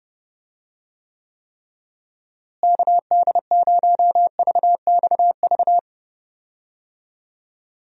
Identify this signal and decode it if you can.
Morse code, heard as a keyed tone, about 30 wpm, 700 Hz. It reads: KD0VXV
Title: USA Call Signs - 6 Characters - Repeated 3x - 30wpm